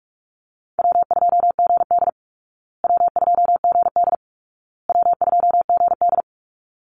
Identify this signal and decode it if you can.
Morse code, heard as a keyed tone, about 44 wpm, 700 Hz. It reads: W2GD W2GD W2GD
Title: Easy Call Signs Repeated 3x - 45wpm